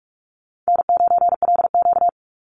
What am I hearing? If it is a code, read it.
Morse code, heard as a keyed tone, about 44 wpm, 700 Hz. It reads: N9LQ
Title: Easy Call Signs Repeated 3x - 45wpm